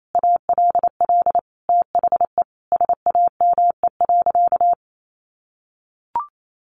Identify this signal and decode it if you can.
Morse code, heard as a keyed tone, about 28 wpm, 700 Hz. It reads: ALL THE SAME.
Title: Sentences from Top 1500 Words 28wpm